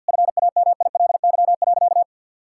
Morse code, heard as a keyed tone, about 50 wpm, 700 Hz. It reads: FAMILY.
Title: Sentences from Top 1000 Words - Review Part 6 - 50wpm